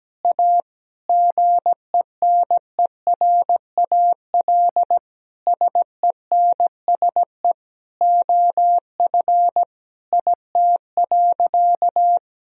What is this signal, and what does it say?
Morse code, heard as a keyed tone, about 17 wpm, 700 Hz. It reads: A GENERAL SENSE OF IT.
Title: Sentences from Top 1000 Words - Review Part 4 - 17wpm